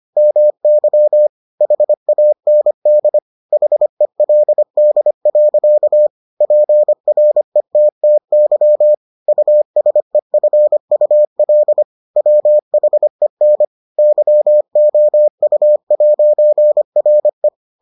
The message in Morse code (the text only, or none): MY HAND HELD. PRETTY USEFUL WHEN YOU'RE